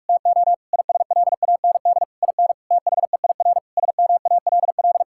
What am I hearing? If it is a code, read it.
Morse code, heard as a keyed tone, about 41 wpm, 700 Hz. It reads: TO ISLAND IN THEIR SMALL
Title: Sentences from Top 1000 Words - Review Part 6 - 45wpm